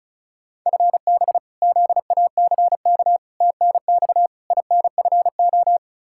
Morse code, heard as a keyed tone, about 36 wpm, 700 Hz. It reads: FB ZACK TNX INFO
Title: Practice QSO - Exchange 3 - More Info - 35wpm